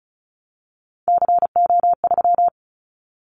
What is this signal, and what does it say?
Morse code, heard as a keyed tone, about 36 wpm, 700 Hz. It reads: CO3
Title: Callsign Prefixes for Top 100 DXCC - 35wpm